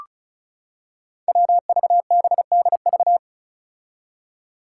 Morse code, heard as a keyed tone, about 36 wpm, 700 Hz. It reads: WVBDV
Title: Five Challenging Letter-Number Combinations 35wpm